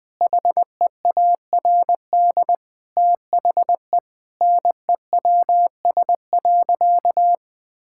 Morse code, heard as a keyed tone, about 20 wpm, 700 Hz. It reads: HEARD THE NEWS.
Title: Sentences with 5-Word Phrases - Spoken Beginning - 20wpm